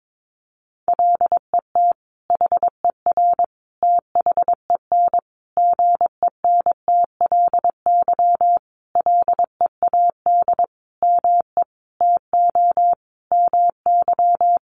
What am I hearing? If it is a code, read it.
Morse code, heard as a keyed tone, about 22 wpm, 700 Hz. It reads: LET HER THEN GENTLY LEAD ME TO MY
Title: Sentences from Top 1500 Words 22wpm